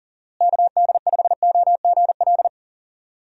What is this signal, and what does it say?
Morse code, heard as a keyed tone, about 40 wpm, 700 Hz. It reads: KD5OGL
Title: USA Call Signs - 6 Characters - Repeated 3x - 40wpm